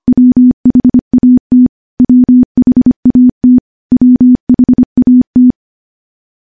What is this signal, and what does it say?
Morse code, heard as a keyed tone, about 25 wpm, 250 Hz. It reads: WHAT WHAT WHAT
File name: morse / morse-training-25WPM